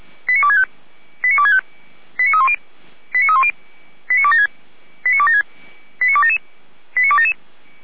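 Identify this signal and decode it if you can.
Five-tone selective call (ZVEI), heard as quick runs of five tones, tones of about 70 ms. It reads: 89265 89265 89310 89310 89276 89277 89277
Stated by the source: Five-tone ZVEI selective call received by SDR, from the audio archive of feuerwehr-orlamuende.org fire brigade.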